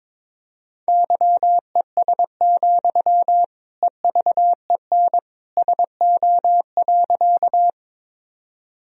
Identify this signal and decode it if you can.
Morse code, heard as a keyed tone, about 21 wpm, 700 Hz. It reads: YES, EVEN SO.
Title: Sentences from Top 1000 Words - Review Part 1 - 22wpm